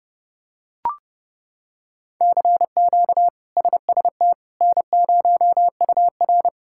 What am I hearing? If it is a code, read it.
Morse code, heard as a keyed tone, about 30 wpm, 700 Hz. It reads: CQ SST N0UR
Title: K1USN SST QSO Practice - 30wpm